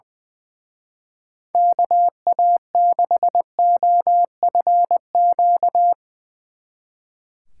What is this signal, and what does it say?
Morse code, heard as a keyed tone, about 20 wpm, 700 Hz. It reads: KA6OFQ
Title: Speed Racing - USA 2x3 Call Signs 20wpm